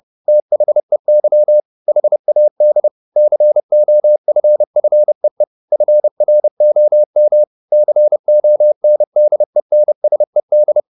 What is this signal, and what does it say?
Morse code, heard as a keyed tone, about 30 wpm, 600 Hz. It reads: THEY HAD COFFEE FROM CONDENSED
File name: the old man and the sea / the old man and the sea 1 30wpm